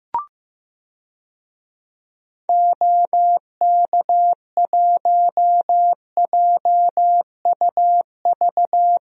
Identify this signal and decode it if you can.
Morse code, heard as a keyed tone, about 15 wpm, 700 Hz. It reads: OK1JUV
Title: DXCC-100 Call Signs - 6 Characters - 15wpm